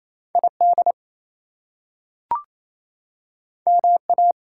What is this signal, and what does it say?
Morse code, heard as a keyed tone, about 28 wpm, 700 Hz. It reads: ID MA
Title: US States and Canadian Provinces 28wpm